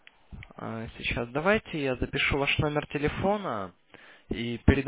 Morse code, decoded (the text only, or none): none